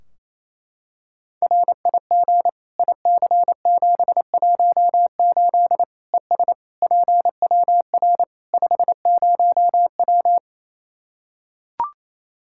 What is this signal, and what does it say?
Morse code, heard as a keyed tone, about 28 wpm, 700 Hz. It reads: RIG IC718 ES PWR 50W
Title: Practice QSO - Exchange 2 - Info - 28wpm